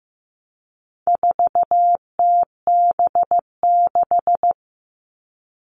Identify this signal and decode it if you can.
Morse code, heard as a keyed tone, about 15 wpm, 700 Hz. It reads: 4TB6
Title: All Cut Numbers - 4-Digit 15wpm